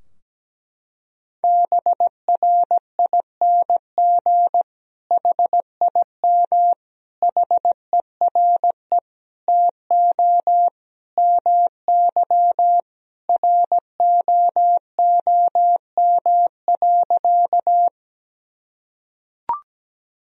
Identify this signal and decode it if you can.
Morse code, heard as a keyed tone, about 17 wpm, 700 Hz. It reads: BRING HIM HERE TO MY ROOM.
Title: Sentences from Top 1000 Words - Review Part 3 - 17wpm